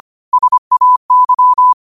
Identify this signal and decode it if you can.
Morse code, heard as a keyed tone, about 25 wpm, 1000 Hz. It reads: SAY
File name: qotdfiles / 25wpm